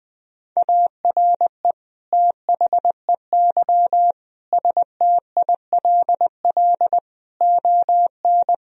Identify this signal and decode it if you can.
Morse code, heard as a keyed tone, about 20 wpm, 700 Hz. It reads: ARE THEY STILL ON
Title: Sentences from Top 1000 Words - Review Part 3 - 20wpm